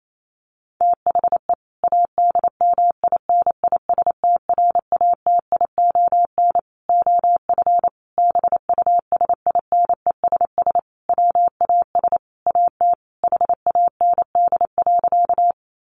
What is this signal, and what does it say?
Morse code, heard as a keyed tone, about 28 wpm, 700 Hz. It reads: THE ADMINISTRATION OF BUSINESS WAS AT HAND.
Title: Sentences from Top 1300 Words 28wpm